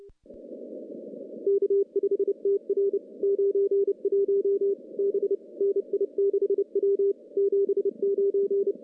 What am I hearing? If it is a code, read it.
Morse code, heard as a keyed tone, about 27 wpm, 410 Hz. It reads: K5TR 91 B NI6W 79
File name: contests / CW-SS-k5tr- -1400